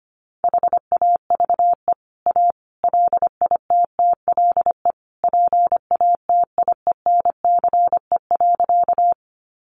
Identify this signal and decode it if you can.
Morse code, heard as a keyed tone, about 25 wpm, 700 Hz. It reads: HAVE A LITTLE PATIENCE.
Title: Sentences from Top 1300 Words 25wpm